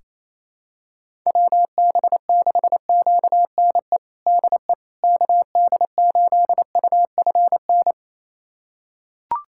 Morse code, heard as a keyed tone, about 28 wpm, 700 Hz. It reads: WB6QNE DE KD8UFN
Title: USA Callsign DE Callsign - 6 Characters - 28wpm